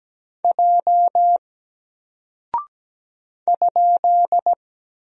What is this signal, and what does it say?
Morse code, heard as a keyed tone, about 17 wpm, 700 Hz. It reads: J ?
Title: Single Character (Letter:Number:Punctuation:Prosign) 17wpm